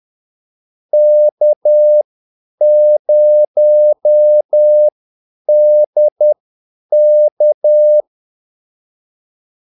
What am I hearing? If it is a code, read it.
Morse code, heard as a keyed tone, about 10 wpm, spaced as for 8 wpm, 600 Hz. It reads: K0DK
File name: Callsigns-10-8wpm